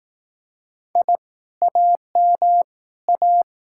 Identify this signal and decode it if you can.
Morse code, heard as a keyed tone, about 18 wpm, 700 Hz. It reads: I AM A
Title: Speed-Racing Common QSO Phrases Encore - 15wpm